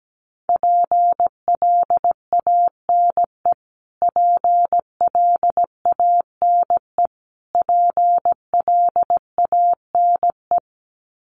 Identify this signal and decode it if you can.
Morse code, heard as a keyed tone, about 17 wpm, 700 Hz. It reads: PLANE PLANE PLANE
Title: Top 500 Words Repeated 17wpm